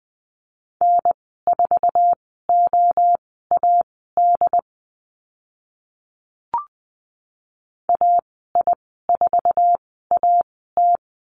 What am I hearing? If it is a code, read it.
Morse code, heard as a keyed tone, about 20 wpm, spaced as for 15 wpm, 700 Hz. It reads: N4OAD AI4AT
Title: 069 Lesson 12g - Letter d - Call Signs 20wpm